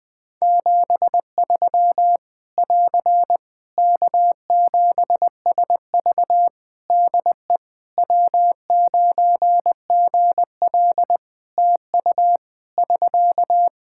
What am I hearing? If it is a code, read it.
Morse code, heard as a keyed tone, about 20 wpm, 700 Hz. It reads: 73 <AR> K7SV DE W9GL TU <SK>
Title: Practice QSO - Exchange 4 - Ending - 20wpm